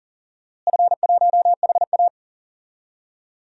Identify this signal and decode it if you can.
Morse code, heard as a keyed tone, about 40 wpm, 700 Hz. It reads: F1HA
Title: DXCC-100 Call Signs - 4 Characters - Repeated 2x - 40wpm